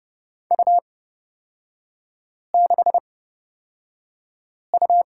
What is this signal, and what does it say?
Morse code, heard as a keyed tone, about 29 wpm, spaced as for 5 wpm, 700 Hz. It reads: U6U
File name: Lesson 37e - Letter x - Three Characters 30wpm